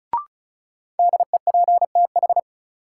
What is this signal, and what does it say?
Morse code, heard as a keyed tone, about 34 wpm, 700 Hz. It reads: DEPTH
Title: Sentences from Top 1200 Words - New Word Drill - 35wpm